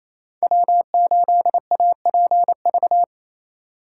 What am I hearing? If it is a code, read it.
Morse code, heard as a keyed tone, about 29 wpm, 700 Hz. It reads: W8APV
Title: Speed Racing - USA 1x3 Call Signs 28wpm